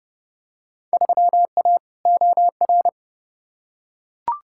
Morse code, heard as a keyed tone, about 30 wpm, 700 Hz. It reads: 3A OR
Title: Field Day Exchanges - 30wpm